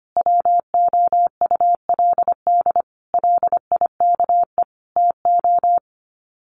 Morse code, heard as a keyed tone, about 25 wpm, 700 Hz. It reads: WOULD LIKE TO
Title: Sets of 5 Words Encore 25wpm